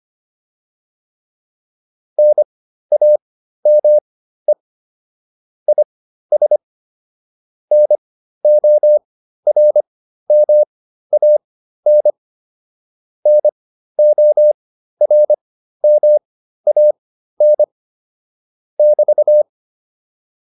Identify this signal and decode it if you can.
Morse code, heard as a keyed tone, about 25 wpm, spaced as for 13 wpm, 600 Hz. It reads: NAME IS NORMAN NORMAN <BT>